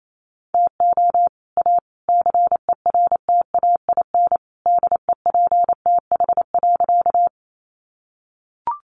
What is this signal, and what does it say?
Morse code, heard as a keyed tone, about 28 wpm, 700 Hz. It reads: TO A CERTAIN DEPTH.
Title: Sentences from Top 1200 Words 28wpm